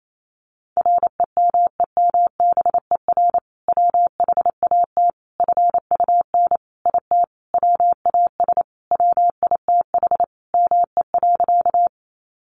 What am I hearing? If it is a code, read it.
Morse code, heard as a keyed tone, about 28 wpm, 700 Hz. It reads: REMEMBER WHAT FUN IT WAS WITH ME.
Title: Sentences from Top 700 Words 28wpm